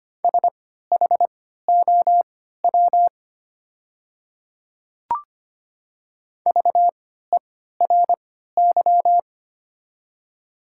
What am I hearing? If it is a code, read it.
Morse code, heard as a keyed tone, about 25 wpm, spaced as for 14 wpm, 700 Hz. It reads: SHOW VERY
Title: 3x Character Spacing - Top 200 Words - 25wpm